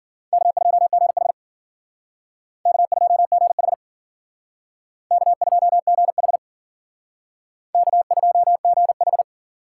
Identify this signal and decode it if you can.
Morse code, heard as a keyed tone, about 52 wpm, 700 Hz. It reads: K2GH K2GH K2GH K2GH